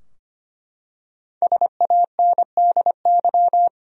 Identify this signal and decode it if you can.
Morse code, heard as a keyed tone, about 25 wpm, 700 Hz. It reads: SANDY
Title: USA Common Names - Max 5 Letters - 25wpm